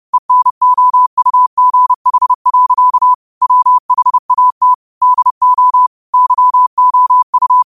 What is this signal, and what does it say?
Morse code, heard as a keyed tone, about 30 wpm, 1000 Hz. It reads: ENOUGH. WHAT DO YOU
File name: qotdfiles / 30wpm